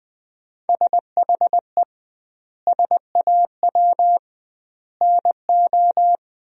Morse code, heard as a keyed tone, about 20 wpm, 700 Hz. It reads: SHE SAW NO
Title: 2x Word-Spacing - Sentences from Top 300 Words - 20wpm